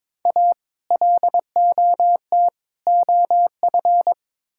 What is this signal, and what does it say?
Morse code, heard as a keyed tone, about 22 wpm, 700 Hz. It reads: A LOT OF
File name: practice / 22wpm / Build A Phrase - 5-Word Phrases - 22wpm